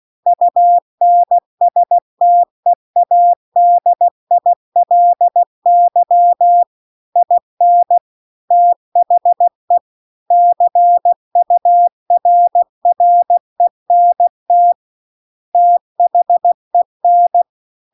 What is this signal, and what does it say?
Morse code, heard as a keyed tone, about 16 wpm, 700 Hz. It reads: UNSTEADILY IN THE CURRENT THEN
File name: CW-The-Big-Twoheaded-River-2-16WPM